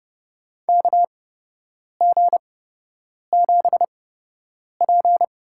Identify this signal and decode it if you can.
Morse code, heard as a keyed tone, about 30 wpm, spaced as for 8 wpm, 700 Hz. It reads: KG7P
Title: USA Call Signs - 6 Characters - 8x Character Spacing - 30wpm